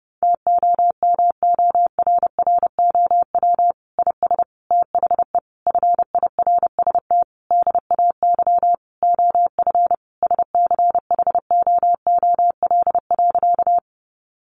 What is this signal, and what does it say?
Morse code, heard as a keyed tone, about 30 wpm, 700 Hz. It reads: TOMORROW IS THE FIRST DAY OF SCHOOL.
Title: Additional Easy Phrases 30wpm